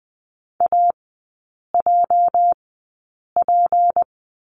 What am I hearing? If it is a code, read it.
Morse code, heard as a keyed tone, about 20 wpm, spaced as for 8 wpm, 690 Hz. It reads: AJP
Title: 205 Lesson 33e - Letter j - Three Characters 20wpm